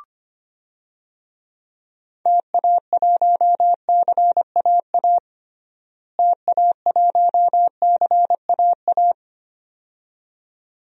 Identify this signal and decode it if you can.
Morse code, heard as a keyed tone, about 25 wpm, 700 Hz. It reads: TA1CAA TA1CAA
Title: DXCC-100 Call Signs - 6 Characters - Repeated 2x - 25wpm